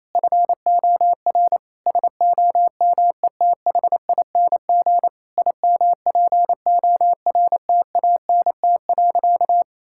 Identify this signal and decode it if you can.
Morse code, heard as a keyed tone, about 28 wpm, 700 Hz. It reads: FOR SOMETHING IMPORTANT.
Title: Sentences from Top 1000 Words - Review Part 6 - 28wpm